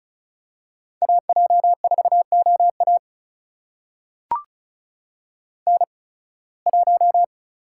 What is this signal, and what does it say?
Morse code, heard as a keyed tone, about 36 wpm, spaced as for 23 wpm, 700 Hz. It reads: AJ4OA N1
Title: USA Call Signs - 5 Characters - 8x Character Spacing - 35wpm